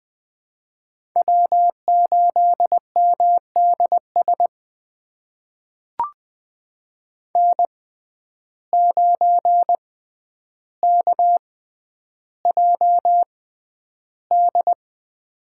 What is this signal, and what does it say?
Morse code, heard as a keyed tone, about 20 wpm, spaced as for 10 wpm, 700 Hz. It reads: W8MDS N9KJD
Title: USA Call Signs - 5 Characters - 6x Character Spacing - 20wpm